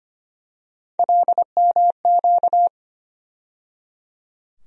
Morse code, heard as a keyed tone, about 25 wpm, 700 Hz. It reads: LMQ
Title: USA Call Sign Postfixes - 3 Letters 25wpm